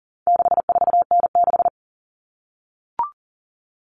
Morse code, heard as a keyed tone, about 40 wpm, 700 Hz. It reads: 64N6
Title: All Cut Numbers - 4-Digit 40wpm